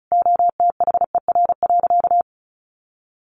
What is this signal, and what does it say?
Morse code, heard as a keyed tone, about 34 wpm, 700 Hz. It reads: OTHER.
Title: Easy Phrases 35wpm